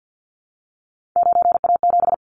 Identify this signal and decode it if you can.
Morse code, heard as a keyed tone, about 50 wpm, 700 Hz. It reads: 9A7
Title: Callsign Prefixes for Top 50 DXCC - 50wpm